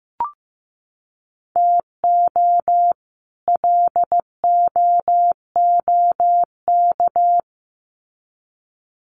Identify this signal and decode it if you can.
Morse code, heard as a keyed tone, about 15 wpm, 700 Hz. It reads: TO LOOK
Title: Sets of 2 Words 15wpm